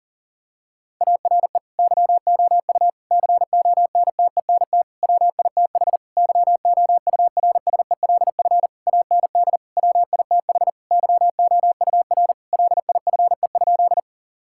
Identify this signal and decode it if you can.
Morse code, heard as a keyed tone, about 37 wpm, 700 Hz. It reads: ARE YOU CONTENT WITH YOURSELF AND WITH YOUR LIFE?
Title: Sentences from Top 1000 Words - Review Part 6 - 40wpm